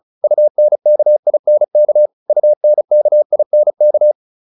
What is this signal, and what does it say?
Morse code, heard as a keyed tone, about 34 wpm, 600 Hz. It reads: UNKINK UNKINK